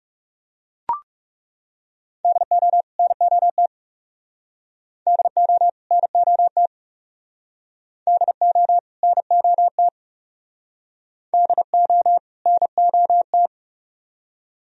Morse code, heard as a keyed tone, about 36 wpm, 700 Hz. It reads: DO NOT DO NOT DO NOT DO NOT
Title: Speed-Racing - Sets of 2 Words 30wpm